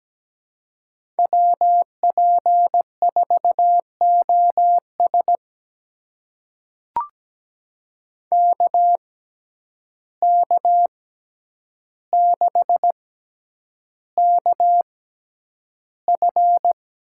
Morse code, heard as a keyed tone, about 17 wpm, spaced as for 9 wpm, 700 Hz. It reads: WP4OS KK6KF
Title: USA Call Signs - 5 Characters - 6x Character Spacing - 17wpm